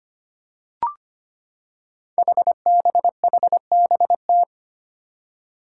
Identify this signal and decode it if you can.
Morse code, heard as a keyed tone, about 25 wpm, 700 Hz. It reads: HBHBT
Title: Five Challenging Letter-Number Combinations 25wpm